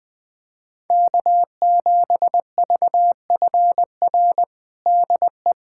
Morse code, heard as a keyed tone, about 20 wpm, 700 Hz. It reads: K7VFR DE
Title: USA Callsign DE Callsign - 5 Characters - 20wpm